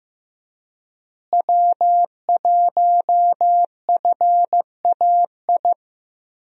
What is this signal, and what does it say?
Morse code, heard as a keyed tone, about 15 wpm, 700 Hz. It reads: W1FAI
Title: USA Call Signs - 5 Characters - Repeated 2x - 15wpm